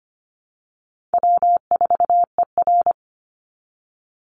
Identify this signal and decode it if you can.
Morse code, heard as a keyed tone, about 25 wpm, 700 Hz. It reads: W4ER
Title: Easy Call Signs Repeated 3x - 25wpm